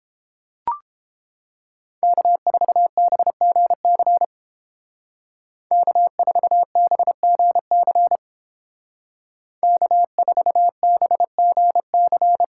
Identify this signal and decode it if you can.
Morse code, heard as a keyed tone, about 31 wpm, 700 Hz. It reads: K4BGC K4BGC K4BGC